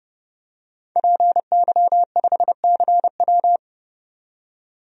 Morse code, heard as a keyed tone, about 32 wpm, 700 Hz. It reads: PY5CW
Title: DXCC-100 Call Signs - 5 Characters - 30wpm